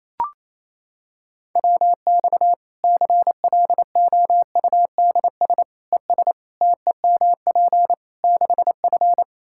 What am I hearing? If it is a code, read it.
Morse code, heard as a keyed tone, about 28 wpm, 700 Hz. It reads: WX CLOUDS ES TEMP 6F
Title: Practice QSO - Exchange 2 - Info - 28wpm